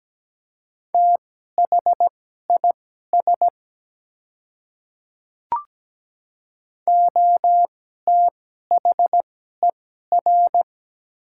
Morse code, heard as a keyed tone, about 17 wpm, spaced as for 12 wpm, 700 Hz. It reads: THIS OTHER
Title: 2x Character Spacing - Top 100 Words - 17wpm